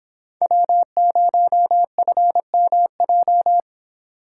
Morse code, heard as a keyed tone, about 31 wpm, 700 Hz. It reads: W0FMJ